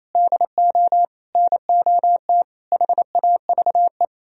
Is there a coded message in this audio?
Yes: Morse code from 0.1 to 4.0 s, heard as a keyed tone, about 37 wpm, 700 Hz. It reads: DO NOT HAVE